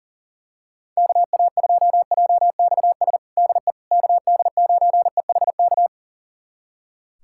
Morse code, heard as a keyed tone, about 37 wpm, 700 Hz. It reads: KA2JXS DE KD9EHK
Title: USA Callsign DE Callsign - 6 Characters - 40wpm